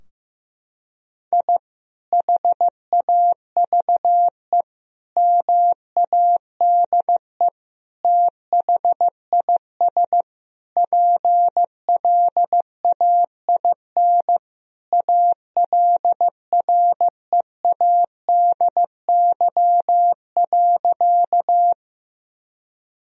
Morse code, heard as a keyed tone, about 15 wpm, 700 Hz. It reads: I HAVE MADE THIS PLAIN ALREADY.
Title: Sentences from Top 1000 Words - Review Part 2 - 15wpm